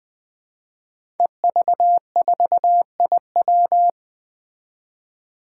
Morse code, heard as a keyed tone, about 20 wpm, 700 Hz. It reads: EV4IW
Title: Five Challenging Letter-Number Combinations 20wpm